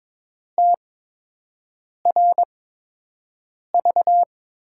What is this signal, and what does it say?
Morse code, heard as a keyed tone, about 22 wpm, spaced as for 6 wpm, 700 Hz. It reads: TRV
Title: USA Call Signs - 5 Characters - 8x Character Spacing - 22wpm